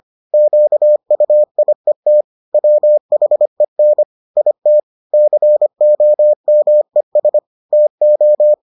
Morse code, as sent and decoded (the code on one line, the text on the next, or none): --.- ..- .. - / .-- .... -. / .. - / -.-. --- -- . ... / - ---
QUIET WHEN IT COMES TO